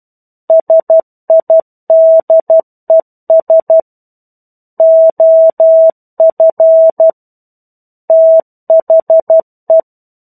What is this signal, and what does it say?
Morse code, heard as a keyed tone, about 12 wpm, 650 Hz. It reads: SIDES OF THE
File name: 01-Chapter-01 12 WPM